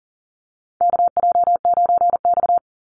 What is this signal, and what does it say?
Morse code, heard as a keyed tone, about 40 wpm, 700 Hz. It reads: KJ9X
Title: NA SOTA Chaser Callsigns 40wpm